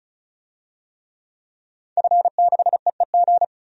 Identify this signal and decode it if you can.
Morse code, heard as a keyed tone, about 34 wpm, 700 Hz. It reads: F6EEG
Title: DXCC-100 Call Signs - 5 Characters - 35wpm